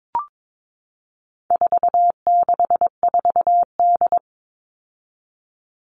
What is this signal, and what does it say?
Morse code, heard as a keyed tone, about 22 wpm, 700 Hz. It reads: 464D
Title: All Cut Numbers - 4-Digit 22wpm